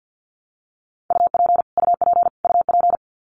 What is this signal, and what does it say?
Morse code, heard as a keyed tone, about 50 wpm, 700 Hz. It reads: UP UP UP
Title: Top 300 Words Repeated 50wpm